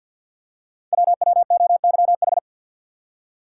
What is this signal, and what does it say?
Morse code, heard as a keyed tone, about 51 wpm, 700 Hz. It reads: WWOYH
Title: Code Groups - 5 Letters - 50wpm